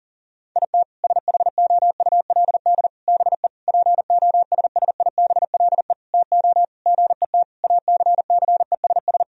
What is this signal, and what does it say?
Morse code, heard as a keyed tone, about 40 wpm, 700 Hz. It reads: IT SHOULD BE POSSIBLE TO GET ACCESS
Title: Sentences from Top 1000 Words - Review Part 4 - 40wpm